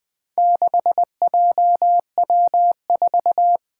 Morse code, heard as a keyed tone, about 20 wpm, 700 Hz. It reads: 6JW4
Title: Four Challenging Letter-Number Combinations 20wpm